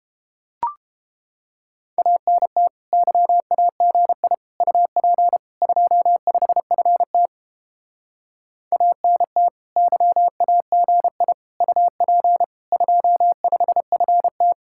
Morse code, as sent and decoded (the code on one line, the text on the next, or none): .- -. - / -.-- .- --. .. / ..- .--. / ..--- ..... ..-. - / .- -. - / -.-- .- --. .. / ..- .--. / ..--- ..... ..-. -
ANT YAGI UP 25FT ANT YAGI UP 25FT